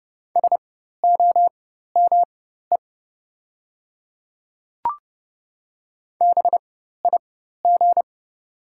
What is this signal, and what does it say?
Morse code, heard as a keyed tone, about 30 wpm, spaced as for 14 wpm, 700 Hz. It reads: SOME BIG
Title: 4x Character Spacing - Top 300 Words - 30wpm